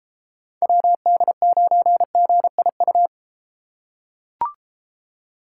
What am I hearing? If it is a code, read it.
Morse code, heard as a keyed tone, about 32 wpm, 700 Hz. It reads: WD9GIU